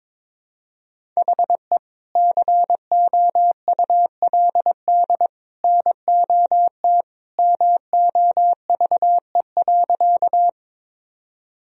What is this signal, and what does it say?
Morse code, heard as a keyed tone, about 22 wpm, 700 Hz. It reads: HE COULD NOT MOVE.
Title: Sentences from Top 1000 Words - Review Part 1 - 22wpm